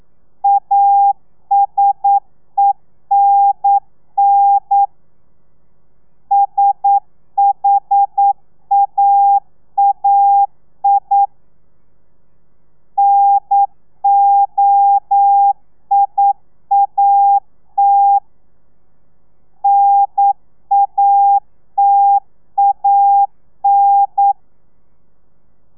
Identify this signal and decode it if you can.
Morse code, heard as a keyed tone, about 9 wpm, 800 Hz. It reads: ASENN SHAAI NOIAT NATAN
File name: telegrafia / i2tfj / Lez2 40